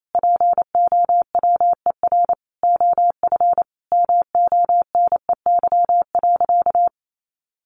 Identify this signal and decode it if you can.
Morse code, heard as a keyed tone, about 28 wpm, 700 Hz. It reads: POWER OF MONEY.
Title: Sentences from Top 1100 Words 28wpm